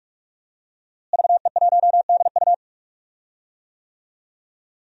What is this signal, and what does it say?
Morse code, heard as a keyed tone, about 46 wpm, 700 Hz. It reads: VE1DU
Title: DXCC-100 Call Signs - 5 Characters - Repeated 2x - 45wpm